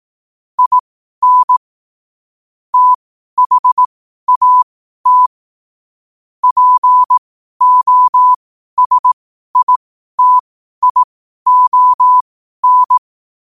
Morse code, heard as a keyed tone, about 18 wpm, spaced as for 12 wpm, 1000 Hz. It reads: IN THAT POSITION
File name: qotdfiles / 12wpm